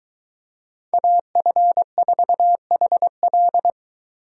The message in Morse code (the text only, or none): AF4HL